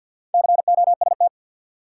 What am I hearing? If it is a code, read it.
Morse code, heard as a keyed tone, about 52 wpm, 700 Hz. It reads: COST
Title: Sets of 3 Words - New Word Drill - 50wpm